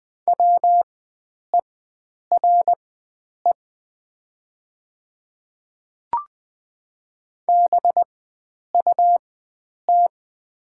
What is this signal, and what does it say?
Morse code, heard as a keyed tone, about 20 wpm, spaced as for 9 wpm, 700 Hz. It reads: WERE BUT
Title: 4x Character Spacing - Top 100 Words - 20wpm